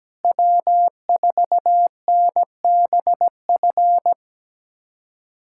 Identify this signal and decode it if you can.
Morse code, heard as a keyed tone, about 17 wpm, 700 Hz. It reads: W4NBF
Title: USA Callsign DE Callsign - 5 Characters - 17wpm